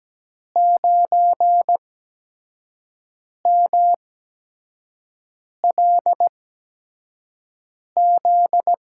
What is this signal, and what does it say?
Morse code, heard as a keyed tone, about 17 wpm, spaced as for 5 wpm, 700 Hz. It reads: 9MLZ
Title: USA Call Signs - 8x Character Spacing - 17wpm